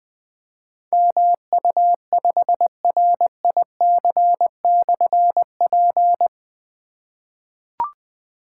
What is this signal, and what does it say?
Morse code, heard as a keyed tone, about 20 wpm, 700 Hz. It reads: MU5RIC/P
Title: Difficult Call Signs 20wpm